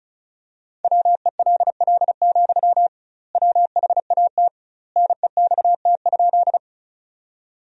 Morse code, heard as a keyed tone, about 34 wpm, 700 Hz. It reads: WELL, WHAT NEXT?
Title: 2x Word-Spacing - Sentences from Top 300 Words - 35wpm